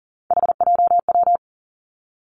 Morse code, heard as a keyed tone, about 40 wpm, 700 Hz. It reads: HJW
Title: Three Challenging Letter-Number Combinations 40wpm